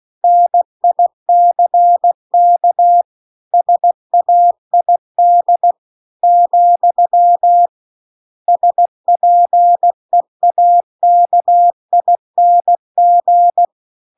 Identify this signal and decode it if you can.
Morse code, heard as a keyed tone, about 16 wpm, 700 Hz. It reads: NICK SAID , SPEAKING